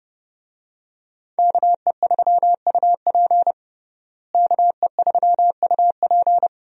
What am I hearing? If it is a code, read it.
Morse code, heard as a keyed tone, about 29 wpm, 700 Hz. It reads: KE3UP KE3UP
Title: USA Call Signs - 5 Characters - Repeated 3x - 30wpm